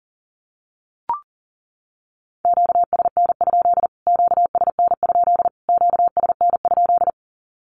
Morse code, heard as a keyed tone, about 40 wpm, 700 Hz. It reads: QSN? QSN? QSN?
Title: Q-Codes Repeated 3x 40wpm